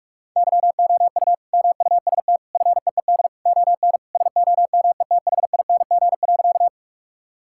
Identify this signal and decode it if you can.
Morse code, heard as a keyed tone, about 46 wpm, 700 Hz. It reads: YOU MUST FEED ON SOMETHING.
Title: Sentences from Top 1600 Words 45wpm